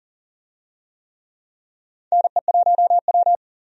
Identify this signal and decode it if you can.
Morse code, heard as a keyed tone, about 40 wpm, 700 Hz. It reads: NE1W
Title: USA Call Signs - 4 Characters - 40wpm